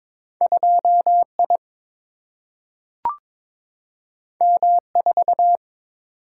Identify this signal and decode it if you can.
Morse code, heard as a keyed tone, about 21 wpm, 700 Hz. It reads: 2 I M 4
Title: Single-Letter Single-Number 22wpm